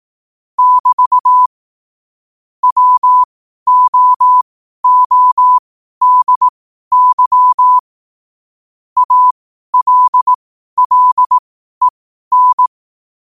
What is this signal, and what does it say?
Morse code, heard as a keyed tone, about 18 wpm, spaced as for 12 wpm, 1000 Hz. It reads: <BT> WOODY ALLEN
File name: qotdfiles / 12wpm